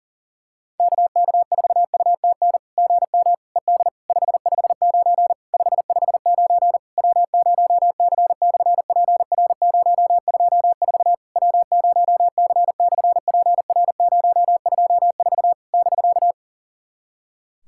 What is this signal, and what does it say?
Morse code, heard as a keyed tone, about 40 wpm, 700 Hz. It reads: KK4UTN GM ED 559 559 W0C/PR024 W0C/PR024 <BK>
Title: SOTA QSO Practice 40wpm